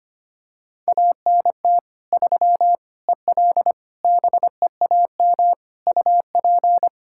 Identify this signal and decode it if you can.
Morse code, heard as a keyed tone, about 25 wpm, 700 Hz. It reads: ANT 3 EL BEAM UP
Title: Practice QSO - Exchange 2 - Info - 25wpm